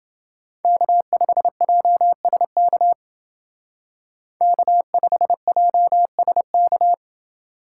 Morse code, heard as a keyed tone, about 29 wpm, 700 Hz. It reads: K5JSK K5JSK